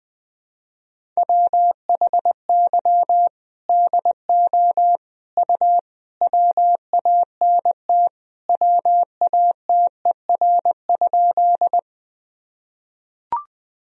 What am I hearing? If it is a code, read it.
Morse code, heard as a keyed tone, about 20 wpm, 700 Hz. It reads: WHY DO U WANT WATER?
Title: Sentences from Top 200 Words and Abbreviations 20wpm